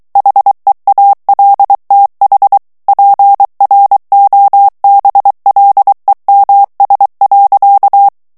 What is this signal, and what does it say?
Morse code, heard as a keyed tone, about 23 wpm, 800 Hz. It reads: HEALTH PROBLEMS.
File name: morse mp3 / ANGLAIS texte clair 1 vitesse 22 mots minute